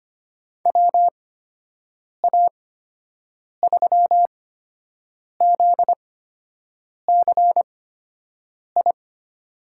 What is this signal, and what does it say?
Morse code, heard as a keyed tone, about 25 wpm, spaced as for 7 wpm, 700 Hz. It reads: WA3ZCI
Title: USA Call Signs - 6 Characters - 8x Character Spacing - 25wpm